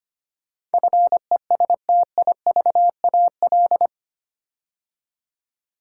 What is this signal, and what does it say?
Morse code, heard as a keyed tone, about 25 wpm, 700 Hz. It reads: FESTIVAL
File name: 8 Letter Words - First 3 Letters Given - 25wpm